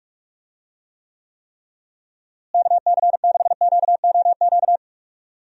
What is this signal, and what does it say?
Morse code, heard as a keyed tone, about 46 wpm, 700 Hz. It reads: KC6QOQ
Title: USA Call Signs - 6 Characters - Repeated 3x - 45wpm